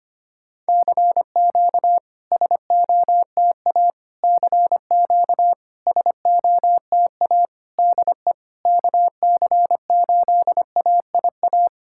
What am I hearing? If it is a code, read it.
Morse code, heard as a keyed tone, about 25 wpm, 700 Hz. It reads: CQ SOTA CQ SOTA DE KC8AIA
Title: SOTA QSO Practice 25wpm